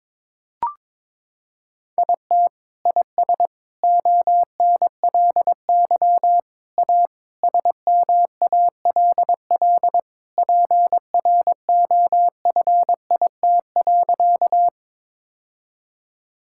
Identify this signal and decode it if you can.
Morse code, heard as a keyed tone, about 22 wpm, 700 Hz. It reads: IT IS ONLY A SMALL PROFIT.
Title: Sentences from Top 1000 Words - Review Part 5 - 22wpm